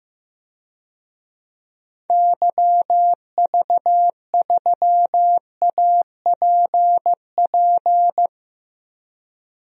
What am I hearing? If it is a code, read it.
Morse code, heard as a keyed tone, about 15 wpm, 700 Hz. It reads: YV3APP
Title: DXCC-100 Call Signs - 6 Characters - Repeated 2x - 15wpm